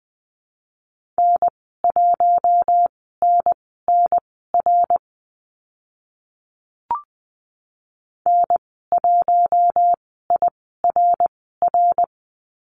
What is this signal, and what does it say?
Morse code, heard as a keyed tone, about 20 wpm, spaced as for 15 wpm, 700 Hz. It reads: N1NNR N1IRR
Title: 055 Lesson 10g - Letter r - Call Signs 20wpm